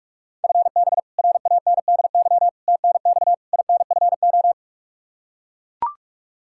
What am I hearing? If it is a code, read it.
Morse code, heard as a keyed tone, about 41 wpm, 700 Hz. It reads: FB RANDY TNX INFO
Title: K1USN SST QSO Practice 45wpm